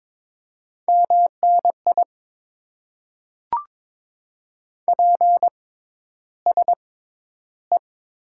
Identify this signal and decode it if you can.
Morse code, heard as a keyed tone, about 22 wpm, spaced as for 11 wpm, 700 Hz. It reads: MNI PSE
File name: Top 90 QSO Elements - 6x Character Spacing - 22wpm